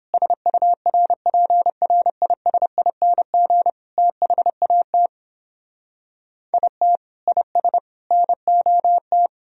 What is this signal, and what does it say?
Morse code, heard as a keyed tone, about 29 wpm, 700 Hz. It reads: SURPRISING THAT IT IS NOT